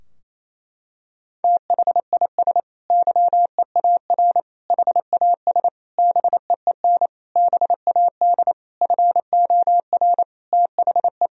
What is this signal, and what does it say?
Morse code, heard as a keyed tone, about 28 wpm, 700 Hz. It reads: THIS YEAR HAS BEEN BAD FOR THE
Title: Sentences from Top 1000 Words - Review Part 3 - 28wpm